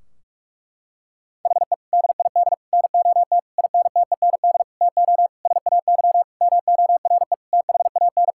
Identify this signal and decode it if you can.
Morse code, heard as a keyed tone, about 46 wpm, 700 Hz. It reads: HE DID NOT INTEND TO SAY MORE THAN
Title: Sentences from Top 1600 Words 45wpm